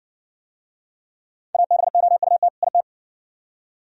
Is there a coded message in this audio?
Yes: Morse code from 1.5 to 2.8 s, heard as a keyed tone, about 57 wpm, 700 Hz. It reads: ABOUT IT